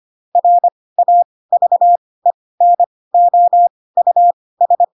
Morse code, heard as a keyed tone, about 25 wpm, spaced as for 18 wpm, 700 Hz. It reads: RAVENOUS
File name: Pinocchio / Pinocchio ch5 12 25 ik1hge